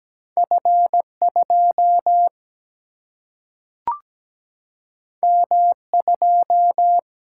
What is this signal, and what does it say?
Morse code, heard as a keyed tone, about 17 wpm, 700 Hz. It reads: F2 M2
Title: Single-Letter Single-Number 17wpm